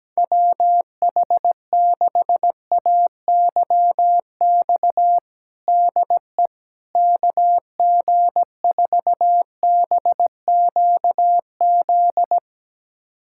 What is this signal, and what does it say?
Morse code, heard as a keyed tone, about 17 wpm, 700 Hz. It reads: WH6AYX DE KG4BQZ
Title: USA Callsign DE Callsign 17wpm